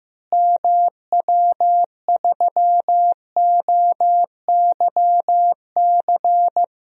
Morse code, heard as a keyed tone, about 15 wpm, 700 Hz. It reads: MW3OYC
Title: DXCC-100 Call Signs - 6 Characters - Repeated 2x - 15wpm